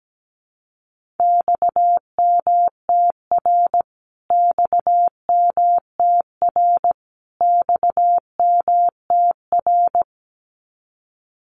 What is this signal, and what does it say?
Morse code, heard as a keyed tone, about 17 wpm, 700 Hz. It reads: XMTR XMTR XMTR
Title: Abbreviations Repeated 3x - 17wpm